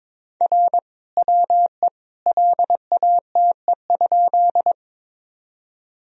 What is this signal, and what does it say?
Morse code, heard as a keyed tone, about 22 wpm, 700 Hz. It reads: R WE LATE?
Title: Sentences from Top 300 Words and Abbreviations 22wpm